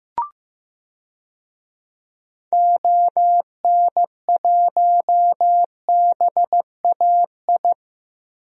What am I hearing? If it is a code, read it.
Morse code, heard as a keyed tone, about 15 wpm, 700 Hz. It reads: ON1BAI
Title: DXCC-100 Call Signs - 6 Characters - Repeated 2x - 15wpm